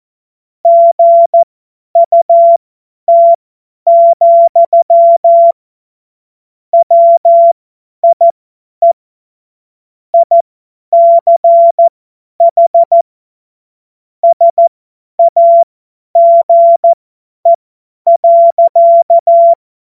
Morse code, heard as a keyed tone, about 14 wpm, spaced as for 10 wpm, 680 Hz. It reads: GUT, WIE ICH SAGE.
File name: Morsekurs / Witze / Witz 10 Juni 2025 50 F70